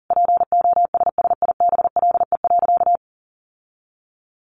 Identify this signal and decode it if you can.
Morse code, heard as a keyed tone, about 40 wpm, 700 Hz. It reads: POSSIBLE.
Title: Sentences from Top 1200 Words 40wpm